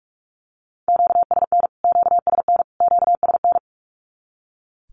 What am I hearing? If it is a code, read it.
Morse code, heard as a keyed tone, about 44 wpm, 700 Hz. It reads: QSN QSN QSN
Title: Q-Codes Repeated 3x 45wpm